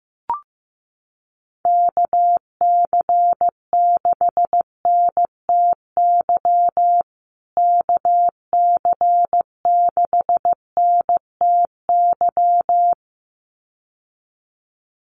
Morse code, heard as a keyed tone, about 15 wpm, 700 Hz. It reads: KC6NTY KC6NTY
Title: USA Call Signs Repeated 2x 15wpm